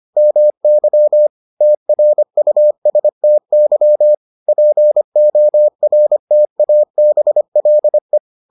Morse code, heard as a keyed tone, about 25 wpm, 600 Hz. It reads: MY TRUSTY PORTABLE